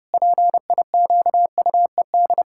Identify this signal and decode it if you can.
Morse code, heard as a keyed tone, about 32 wpm, 700 Hz. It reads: PIQUED
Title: Pangram Challenge v2 - 30wpm